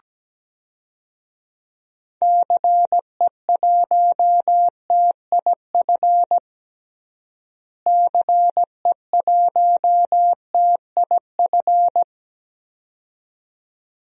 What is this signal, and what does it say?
Morse code, heard as a keyed tone, about 17 wpm, 700 Hz. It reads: CE1TIF CE1TIF
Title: DXCC-100 Call Signs - 6 Characters - Repeated 2x - 17wpm